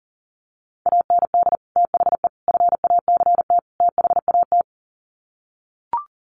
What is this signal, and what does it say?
Morse code, heard as a keyed tone, about 40 wpm, 700 Hz. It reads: AND THE FACT THAT
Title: Sets of 4 Words 40wpm